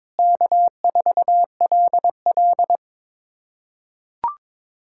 Morse code, heard as a keyed tone, about 22 wpm, 700 Hz. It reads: K4LL
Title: USA Callsign DE Callsign - 4 Characters - 22wpm